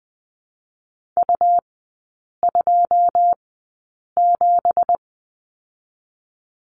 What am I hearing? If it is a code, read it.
Morse code, heard as a keyed tone, about 19 wpm, 700 Hz. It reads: U 2 7
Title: 178 Lesson 29e - Slash - Three Characters 20wpm